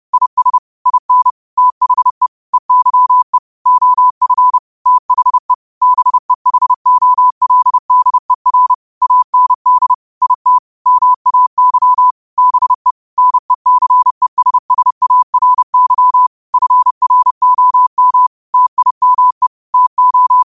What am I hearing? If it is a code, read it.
Morse code, heard as a keyed tone, about 30 wpm, 1000 Hz. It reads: IS IN THE EYE OF THE BEHOLDER AND IT MAY BE NECESSARY FROM TIME TO
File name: qotdfiles / 30wpm